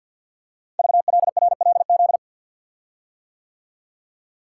Morse code, heard as a keyed tone, about 46 wpm, 700 Hz. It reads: VLRLZ
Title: Code Groups - 5 Letters - 50wpm